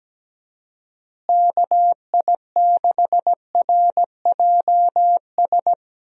Morse code, heard as a keyed tone, about 17 wpm, 700 Hz. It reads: KI6RJS
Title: USA Call Signs - 6 Characters - Repeated 2x - 17wpm